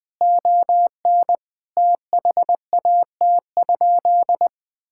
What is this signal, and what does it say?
Morse code, heard as a keyed tone, about 20 wpm, 700 Hz. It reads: ON THAT?
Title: Sentences from Top 1000 Words - Review Part 5 - 20wpm